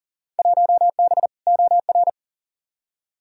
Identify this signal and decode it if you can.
Morse code, heard as a keyed tone, about 40 wpm, 700 Hz. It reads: 1B OR
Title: Field Day Exchanges - 40wpm